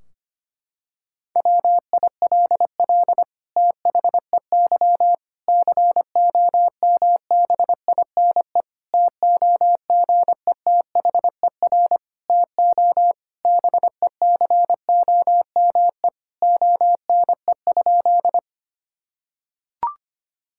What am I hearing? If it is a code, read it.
Morse code, heard as a keyed tone, about 25 wpm, 700 Hz. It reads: WILL THEY COMBINE TOGETHER TO BECOME ONE?
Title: Sentences from Top 1600 Words 25wpm